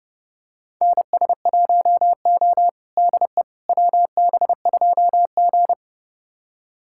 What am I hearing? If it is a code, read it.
Morse code, heard as a keyed tone, about 30 wpm, 700 Hz. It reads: NS1O DE WB2G
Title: USA Callsign DE Callsign - 4 Characters - 30wpm